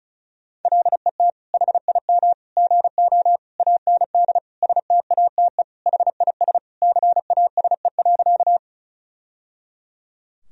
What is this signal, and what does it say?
Morse code, heard as a keyed tone, about 34 wpm, 700 Hz. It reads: LET HIM GO AND STATE HIS CASE.
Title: Sentences from Top 1000 Words - Review Part 2 - 35wpm